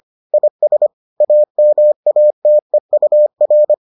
Morse code, heard as a keyed tone, about 25 wpm, 600 Hz. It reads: IS AMATEUR